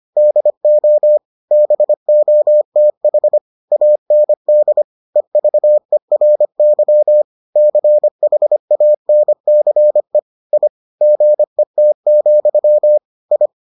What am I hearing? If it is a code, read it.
Morse code, heard as a keyed tone, about 25 wpm, 600 Hz. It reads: DO BOTH AND EVERY CHANCE I GET, I